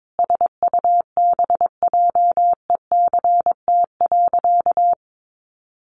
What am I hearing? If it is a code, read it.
Morse code, heard as a keyed tone, about 22 wpm, 700 Hz. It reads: SUBJECT.
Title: Sentences from Top 1500 Words 22wpm